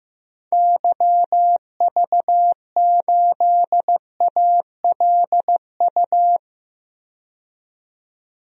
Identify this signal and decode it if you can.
Morse code, heard as a keyed tone, about 15 wpm, 700 Hz. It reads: YV8ALU
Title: DXCC-100 Call Signs - 6 Characters - 15wpm